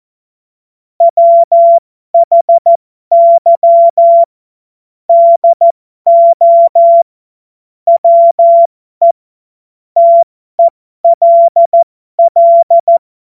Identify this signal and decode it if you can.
Morse code, heard as a keyed tone, about 14 wpm, spaced as for 12 wpm, 680 Hz. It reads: WHY DO WE TELL
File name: Morsekurs / Witze